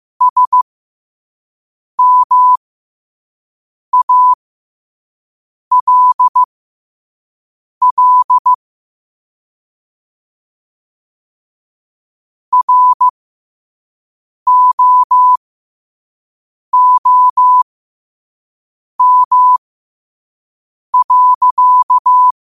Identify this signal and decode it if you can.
Morse code, heard as a keyed tone, about 15 wpm, spaced as for 5 wpm, 1000 Hz. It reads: SMALL ROOM.
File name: 5wpm